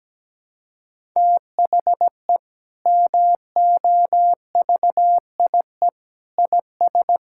Morse code, heard as a keyed tone, about 17 wpm, 700 Hz. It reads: THE MOVIE IS
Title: Sentences from Top 1000 Words - Review Part 5 - 17wpm